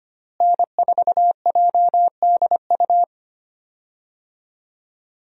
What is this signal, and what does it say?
Morse code, heard as a keyed tone, about 25 wpm, 700 Hz. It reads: N4JDU
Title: USA Call Signs - 5 Characters - Repeated 2x - 25wpm